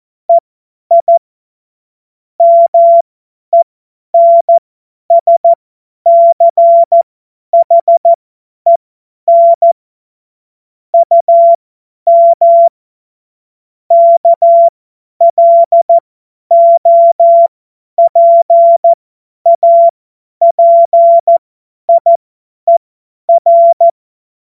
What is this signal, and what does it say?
Morse code, heard as a keyed tone, about 14 wpm, spaced as for 10 wpm, 680 Hz. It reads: EI MENSCHEN UM KLOPAPIER
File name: Morsekurs / Witze / Witz 29. Juni 2021, 50 F70